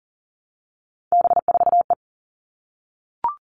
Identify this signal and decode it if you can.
Morse code, heard as a keyed tone, about 40 wpm, 700 Hz. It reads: B4E
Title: All Cut Numbers - 3-Digit 40wpm